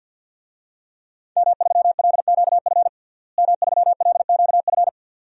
Morse code, heard as a keyed tone, about 50 wpm, 700 Hz. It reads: M3LQF M3LQF
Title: DXCC-100 Call Signs - 5 Characters - Repeated 2x - 50wpm